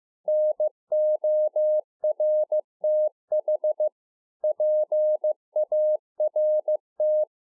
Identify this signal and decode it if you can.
Morse code, heard as a keyed tone, about 15 wpm, 610 Hz. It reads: NORTH PART